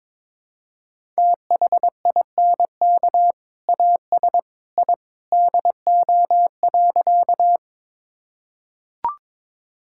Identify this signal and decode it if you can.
Morse code, heard as a keyed tone, about 22 wpm, 700 Hz. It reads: THINK AS I DO.
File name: practice / 22wpm / Sentences with 2-Word Phrases - 22wpm